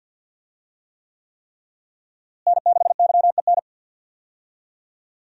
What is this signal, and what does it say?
Morse code, heard as a keyed tone, about 47 wpm, 700 Hz. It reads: N6YEN
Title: USA Call Signs - 5 Characters - Repeated 3x - 50wpm